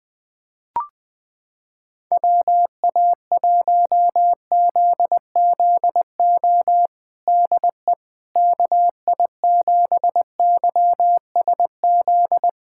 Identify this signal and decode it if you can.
Morse code, heard as a keyed tone, about 20 wpm, 700 Hz. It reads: WA1ZZO DE KI7YSZ
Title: USA Callsign DE Callsign - 6 Characters - 20wpm